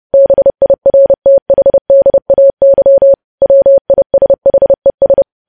Morse code, heard as a keyed tone, about 30 wpm, 550 Hz. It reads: BIRTHDAY WISHES
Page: news headlines in morse code at 30 wpm